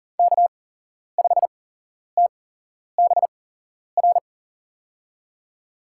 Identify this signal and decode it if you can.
Morse code, heard as a keyed tone, about 40 wpm, spaced as for 11 wpm, 700 Hz. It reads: K5TBR
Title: USA Call Signs - 8x Character Spacing - 40wpm